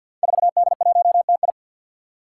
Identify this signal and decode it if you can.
Morse code, heard as a keyed tone, about 51 wpm, 700 Hz. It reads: 4D1TI
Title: Five Challenging Letter-Number Combinations 50wpm